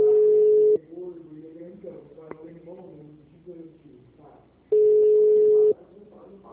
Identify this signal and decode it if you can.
Ringback tone (Europe, 425 Hz), with speech.
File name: out- -2006- - - .259